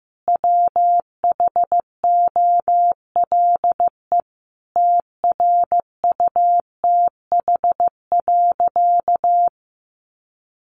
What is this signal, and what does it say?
Morse code, heard as a keyed tone, about 14 wpm, 700 Hz. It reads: WHOLE TRUTH.
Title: Sentences from Top 800 Words 15wpm